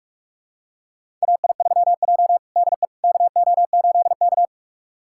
Morse code, heard as a keyed tone, about 46 wpm, 700 Hz. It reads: AI3J DE KO8K
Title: USA Callsign DE Callsign - 4 Characters - 45wpm